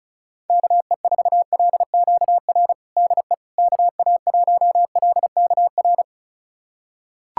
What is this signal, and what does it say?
Morse code, heard as a keyed tone, about 36 wpm, 700 Hz. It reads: KE4LQR DE KA1LKR
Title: USA Callsign DE Callsign - 6 Characters - 35wpm